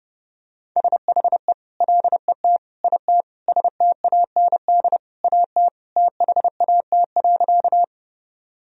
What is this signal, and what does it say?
Morse code, heard as a keyed tone, about 30 wpm, 700 Hz. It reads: SHE LET IT STAND AT THAT.
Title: Sentences from Top 1000 Words - Review Part 1 - 30wpm